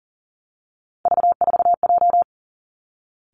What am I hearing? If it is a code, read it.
Morse code, heard as a keyed tone, about 43 wpm, 700 Hz. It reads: V4J
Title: Three Challenging Letter-Number Combinations 40wpm